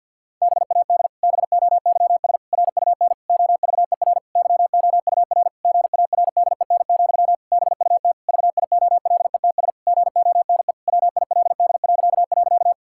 Morse code, heard as a keyed tone, about 50 wpm, 700 Hz. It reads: BAD BOYS RUN OVER YOUR GARDEN, BUT VIOLETS GONE WILD..
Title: Pangram Challenge v2 - 50wpm